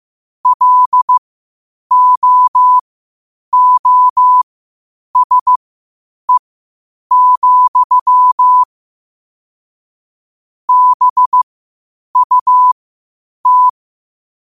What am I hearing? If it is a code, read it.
Morse code, heard as a keyed tone, about 15 wpm, spaced as for 8 wpm, 1000 Hz. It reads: LOOSE, BUT
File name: qotdfiles / 8wpm